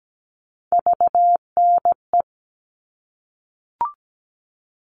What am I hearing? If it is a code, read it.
Morse code, heard as a keyed tone, about 17 wpm, 700 Hz. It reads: VNE
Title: All Cut Numbers - 3-Digit 17wpm